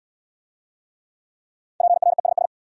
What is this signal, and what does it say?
Morse code, heard as a keyed tone, about 66 wpm, 700 Hz. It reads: PRESS